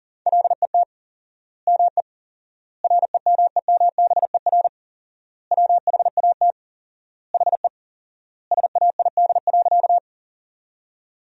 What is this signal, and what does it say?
Morse code, heard as a keyed tone, about 40 wpm, 700 Hz. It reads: LET ME REMEMBER WHAT HE SAID.
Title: 4x Word-Spacing - Sentences from Top 400 Words - 40wpm